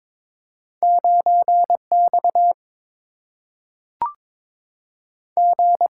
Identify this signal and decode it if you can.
Morse code, heard as a keyed tone, about 22 wpm, 700 Hz. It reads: 9X G
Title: Single-Letter Single-Number 22wpm